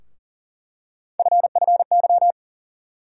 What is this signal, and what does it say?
Morse code, heard as a keyed tone, about 40 wpm, 700 Hz. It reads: FFY
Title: USA Call Sign Postfixes - 3 Letters 40wpm